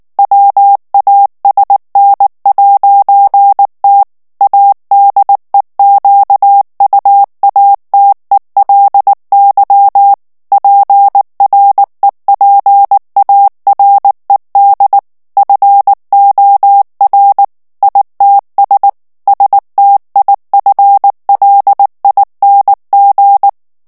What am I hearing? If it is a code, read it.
Morse code, heard as a keyed tone, about 19 wpm, 800 Hz. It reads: WASN'T ADEQUATELY PREPARED FOR ITS STIFLING
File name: ANGLAIS texte clair 2 vitesse 18 mots minute